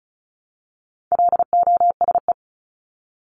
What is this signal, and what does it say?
Morse code, heard as a keyed tone, about 34 wpm, 700 Hz. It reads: LOSE
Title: Top 1001-1250 Words 35wpm